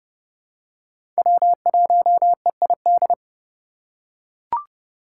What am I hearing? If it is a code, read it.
Morse code, heard as a keyed tone, about 30 wpm, 700 Hz. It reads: W1EID
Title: USA Call Signs - 2x Character Spacing - 30wpm